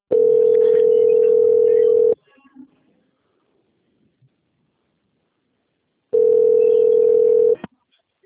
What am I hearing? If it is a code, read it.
Ringback tone (North America), 440 + 480 Hz.